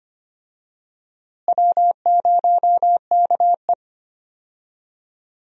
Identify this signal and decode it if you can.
Morse code, heard as a keyed tone, about 25 wpm, 700 Hz. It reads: W0KE
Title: USA Call Signs - 4 Characters - Repeated 2x - 25wpm